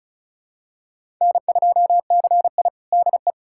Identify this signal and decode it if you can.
Morse code, heard as a keyed tone, about 36 wpm, 700 Hz. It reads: N2CI DE
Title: USA Callsign DE Callsign - 4 Characters - 35wpm